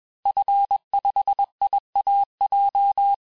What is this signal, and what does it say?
Morse code, heard as a keyed tone, about 21 wpm, 800 Hz. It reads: F5IAJ
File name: teste-FIN-de-QSO-vitesse-20mpm